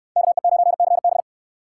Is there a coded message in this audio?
Yes: Morse code from 0.2 to 1.2 s, heard as a keyed tone, about 69 wpm, 700 Hz. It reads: KE9CB